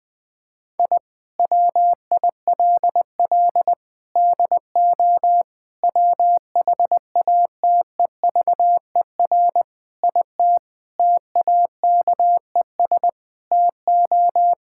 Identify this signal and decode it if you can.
Morse code, heard as a keyed tone, about 20 wpm, 700 Hz. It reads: I WILL DO WHATEVER IT TAKES TO
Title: Sentences with 6-Word Phrases - 20wpm